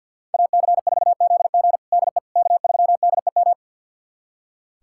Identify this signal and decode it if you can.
Morse code, heard as a keyed tone, about 47 wpm, 700 Hz. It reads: AC4ZG DE K3DEM
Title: USA Callsign DE Callsign - 5 Characters - 50wpm